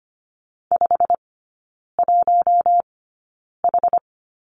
Morse code, heard as a keyed tone, about 25 wpm, spaced as for 9 wpm, 700 Hz. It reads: 51H
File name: course / 25:10wpm / 093 Lesson 16e - Letter c - Three Characters 25wpm